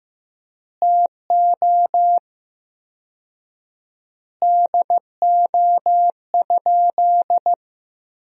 Morse code, heard as a keyed tone, about 15 wpm, 700 Hz. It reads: TO DO?
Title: 4x Word-Spacing - Sentences from Top 400 Words - 15wpm